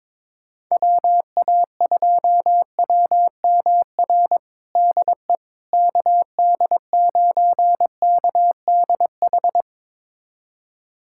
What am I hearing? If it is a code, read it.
Morse code, heard as a keyed tone, about 22 wpm, 700 Hz. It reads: WA2WMR DE KD9KDH
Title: USA Callsign DE Callsign - 6 Characters - 22wpm